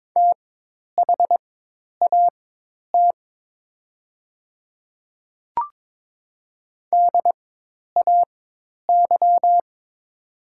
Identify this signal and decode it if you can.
Morse code, heard as a keyed tone, about 23 wpm, spaced as for 10 wpm, 700 Hz. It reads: THAT DAY
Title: 4x Character Spacing - Top 100 Words - 22wpm